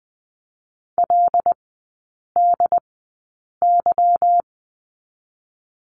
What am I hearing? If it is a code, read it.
Morse code, heard as a keyed tone, about 20 wpm, spaced as for 8 wpm, 700 Hz. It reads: LDY
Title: 145 Lesson 24e - Letter y - Three Characters 20wpm